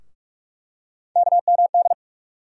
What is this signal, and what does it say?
Morse code, heard as a keyed tone, about 43 wpm, 700 Hz. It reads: KMD
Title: USA Call Sign Postfixes - 3 Letters 45wpm